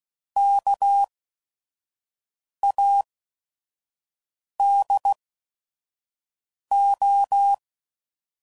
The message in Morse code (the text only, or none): KADO